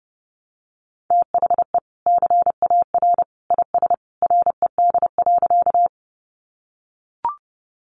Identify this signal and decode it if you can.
Morse code, heard as a keyed tone, about 30 wpm, 700 Hz. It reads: THE CAR IS RED.
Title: Easy Phrases 30wpm